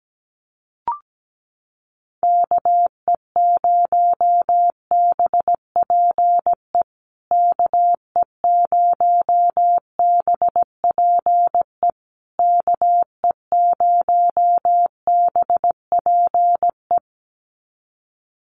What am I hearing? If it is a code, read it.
Morse code, heard as a keyed tone, about 17 wpm, 700 Hz. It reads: KE0BPE KE0BPE KE0BPE
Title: USA Call Signs Repeated 3x 17wpm